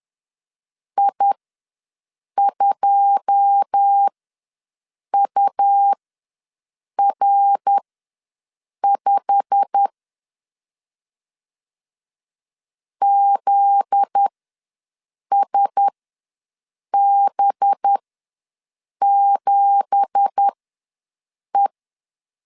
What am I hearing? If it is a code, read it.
Morse code, heard as a keyed tone, about 10 wpm, spaced as for 6 wpm, 800 Hz. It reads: I2UR5 ZSB7E
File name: LEZIONI / 9 2867 ws 10wpm 4215